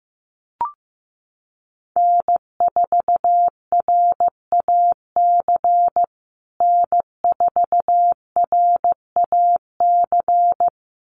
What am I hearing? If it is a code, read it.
Morse code, heard as a keyed tone, about 15 wpm, 700 Hz. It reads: N4RAC N4RAC
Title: USA Call Signs Repeated 2x 15wpm